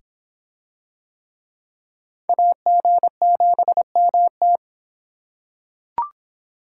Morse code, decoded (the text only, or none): AG7MT